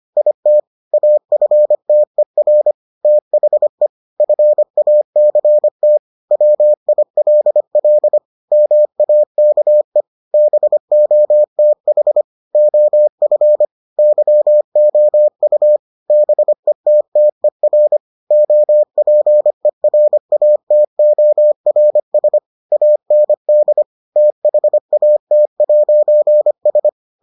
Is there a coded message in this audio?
Yes: Morse code from 0.2 to 26.9 s, heard as a keyed tone, about 25 wpm, 600 Hz. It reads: IT AFTER THE FACT WILL MAKE BOTH OF YOU BETTER OPERATORS AND THAT'S